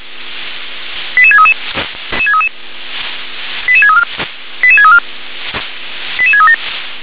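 Five-tone selective call (ZVEI), heard as quick runs of five tones, tones of about 70 ms. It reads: 88633 88634 88634 88637
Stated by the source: ZVEI five-tone selective call for 88730, received by SDR.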